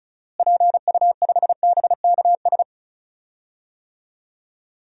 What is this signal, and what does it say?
Morse code, heard as a keyed tone, about 36 wpm, 700 Hz. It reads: PU5BKS
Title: DXCC-100 Call Signs - 6 Characters - Repeated 3x - 35wpm